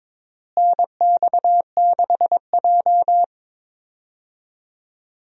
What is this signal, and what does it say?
Morse code, heard as a keyed tone, about 22 wpm, 700 Hz. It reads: NX6J
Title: USA Call Signs - 4 Characters - Repeated 3x - 22wpm